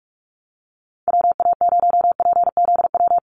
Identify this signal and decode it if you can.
Morse code, heard as a keyed tone, about 44 wpm, 700 Hz. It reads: WA0PZW
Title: USA Call Signs Repeated 3x 45wpm